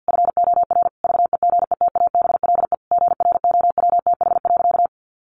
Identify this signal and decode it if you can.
Morse code, heard as a keyed tone, about 50 wpm, 700 Hz. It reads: FOR VEGETABLE GROWTH.
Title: Sentences from Top 1300 Words 50wpm